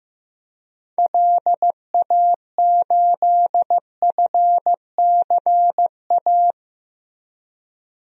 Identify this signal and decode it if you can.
Morse code, heard as a keyed tone, about 15 wpm, 700 Hz. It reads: LA8FCA
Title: DXCC-100 Call Signs - 6 Characters - Repeated 2x - 15wpm